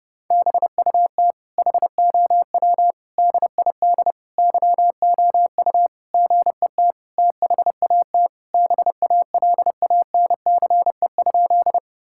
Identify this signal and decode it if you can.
Morse code, heard as a keyed tone, about 28 wpm, 700 Hz. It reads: BUT HOW DID YOU GET THAT BALANCE?
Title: Sentences from Top 1000 Words - Review Part 6 - 30wpm